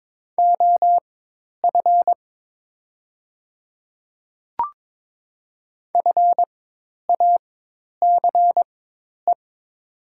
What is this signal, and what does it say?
Morse code, heard as a keyed tone, about 22 wpm, spaced as for 10 wpm, 700 Hz. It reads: OF FACE
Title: 4x Character Spacing - Top 300 Words - 22wpm